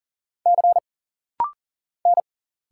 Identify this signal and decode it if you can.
Morse code, heard as a keyed tone, about 40 wpm, 700 Hz. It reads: C N
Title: Single Letters - Rapid-Fire 40wpm